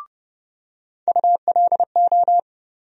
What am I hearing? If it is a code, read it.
Morse code, heard as a keyed tone, about 30 wpm, 700 Hz. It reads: ULO
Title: USA Call Sign Postfixes - 3 Letters 30wpm